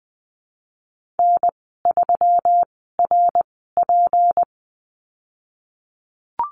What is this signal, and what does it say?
Morse code, heard as a keyed tone, about 20 wpm, spaced as for 15 wpm, 700 Hz. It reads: N3RP
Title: 154 Lesson 25g - Letter p - Call Signs 20wpm